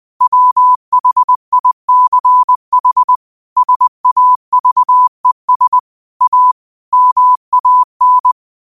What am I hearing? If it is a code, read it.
Morse code, heard as a keyed tone, about 20 wpm, 1000 Hz. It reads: WHICH SAVES A MAN